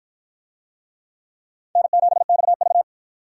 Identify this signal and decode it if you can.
Morse code, heard as a keyed tone, about 50 wpm, 700 Hz. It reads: N7XV